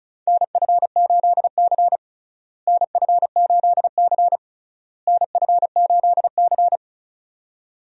Morse code, heard as a keyed tone, about 34 wpm, 700 Hz. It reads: NF8C NF8C NF8C
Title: USA Call Signs - 4 Characters - Repeated 3x - 35wpm